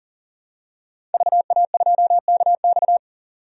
Morse code, heard as a keyed tone, about 40 wpm, 700 Hz. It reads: VA2KX
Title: DXCC-100 Call Signs - 5 Characters - 40wpm